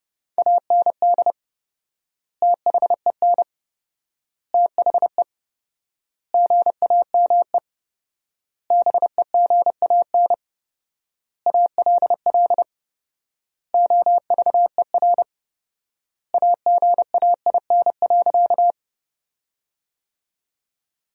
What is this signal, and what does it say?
Morse code, heard as a keyed tone, about 30 wpm, 700 Hz. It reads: AND THEN THE GAME BEGAN ALL OVER AGAIN.
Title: 4x Word-Spacing - Sentences from Top 500 Words - 30wpm